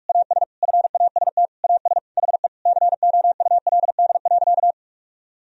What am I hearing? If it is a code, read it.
Morse code, heard as a keyed tone, about 41 wpm, 700 Hz. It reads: AS FAST AS HE COULD.
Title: Sentences from Top 1000 Words - Review Part 2 - 45wpm